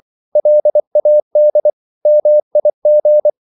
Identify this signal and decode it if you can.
Morse code, heard as a keyed tone, about 24 wpm, 600 Hz. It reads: LAD MIG
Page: Ferie 24 wpm